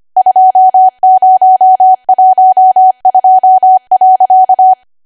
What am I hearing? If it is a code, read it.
Morse code, heard as a keyed tone, about 25 wpm, 750 Hz. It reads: 2012.